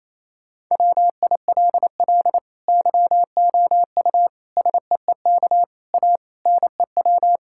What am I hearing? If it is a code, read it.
Morse code, heard as a keyed tone, about 28 wpm, 700 Hz. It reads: WILL YOU SEEK A NEW
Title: Sentences from Top 1600 Words 28wpm